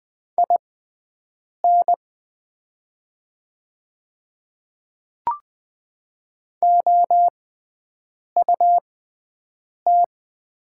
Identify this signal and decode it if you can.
Morse code, heard as a keyed tone, about 20 wpm, spaced as for 7 wpm, 700 Hz. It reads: IN OUT
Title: 6x Character Spacing - Top 100 Words - 20wpm